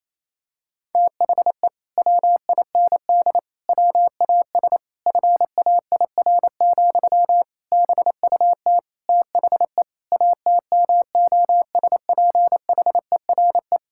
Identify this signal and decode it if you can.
Morse code, heard as a keyed tone, about 28 wpm, 700 Hz. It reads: THE WIND WAS FAIR, BUT THE ATMOSPHERE
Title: Sentences from Top 1600 Words 28wpm